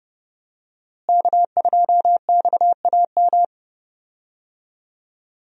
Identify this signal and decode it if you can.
Morse code, heard as a keyed tone, about 30 wpm, 700 Hz. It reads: K2XAM
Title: USA Call Signs - 5 Characters - Repeated 3x - 30wpm